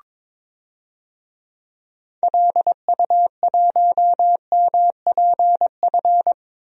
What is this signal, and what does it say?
Morse code, heard as a keyed tone, about 22 wpm, 700 Hz. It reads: LU1MPF
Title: DXCC-100 Call Signs - 6 Characters - Repeated 2x - 22wpm